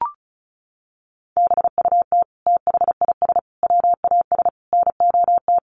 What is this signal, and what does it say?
Morse code, heard as a keyed tone, about 34 wpm, 700 Hz. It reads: BUT THIS WAS NOT
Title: Sentences from Top 600 Words 35wpm